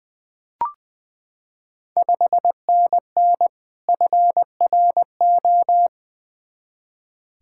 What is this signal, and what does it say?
Morse code, heard as a keyed tone, about 20 wpm, 700 Hz. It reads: 5NN FRO
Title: Ontario QSO Party Exchange 20wpm